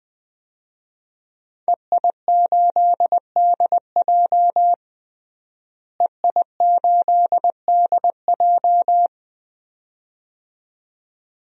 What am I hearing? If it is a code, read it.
Morse code, heard as a keyed tone, about 20 wpm, 700 Hz. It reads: EI8DJ EI8DJ
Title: DXCC-100 Call Signs - 5 Characters - Repeated 2x - 20wpm